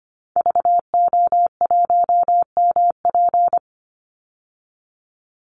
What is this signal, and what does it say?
Morse code, heard as a keyed tone, about 25 wpm, 700 Hz. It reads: VO1MP
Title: Call Signs Repeated 3x - 25wpm